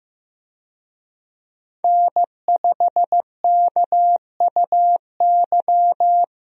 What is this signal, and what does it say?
Morse code, heard as a keyed tone, about 15 wpm, 700 Hz. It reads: N5KUY
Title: USA Call Signs - 5 Characters - Repeated 3x - 15wpm